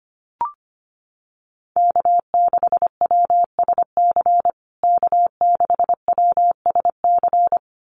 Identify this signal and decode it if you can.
Morse code, heard as a keyed tone, about 25 wpm, 700 Hz. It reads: K6WSC K6WSC
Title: USA Call Signs Repeated 2x 25wpm